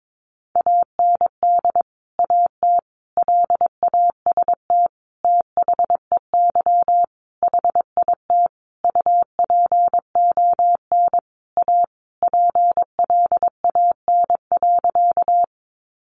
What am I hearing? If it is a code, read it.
Morse code, heard as a keyed tone, about 22 wpm, 700 Hz. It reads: AND AT LAST THEY HIT UPON A PLAN.
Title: Sentences from Top 1200 Words 22wpm